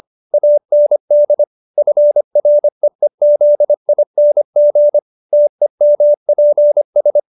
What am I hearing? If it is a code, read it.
Morse code, heard as a keyed tone, about 25 wpm, 590 Hz. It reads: AND FREEZING TEMPS
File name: QSO312 25